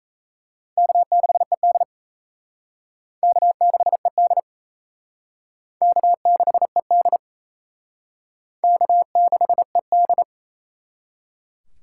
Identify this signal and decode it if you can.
Morse code, heard as a keyed tone, about 34 wpm, 700 Hz. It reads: K6ED K6ED K6ED K6ED
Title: Speed Racing - USA 1x2 Call Signs 28wpm